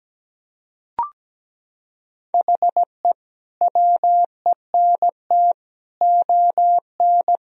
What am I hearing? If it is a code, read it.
Morse code, heard as a keyed tone, about 17 wpm, 700 Hz. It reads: HE WENT ON
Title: Sentences from Top 1000 Words - Review Part 3 - 17wpm